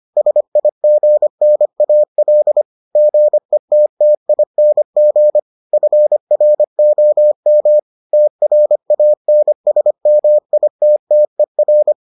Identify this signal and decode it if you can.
Morse code, heard as a keyed tone, about 25 wpm, 600 Hz. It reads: SIGNAL GETTING FROM TRANSMITTER